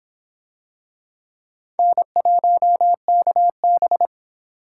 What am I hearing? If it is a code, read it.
Morse code, heard as a keyed tone, about 26 wpm, 700 Hz. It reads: N1KB